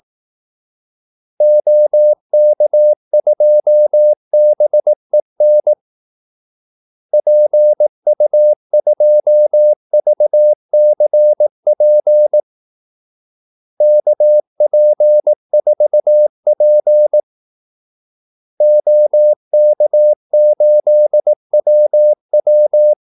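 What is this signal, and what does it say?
Morse code, heard as a keyed tone, about 18 wpm, 600 Hz. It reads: OK2BEN PU2VCP KP4P OK8WW
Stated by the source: Morse call-sign practice at 18 wpm.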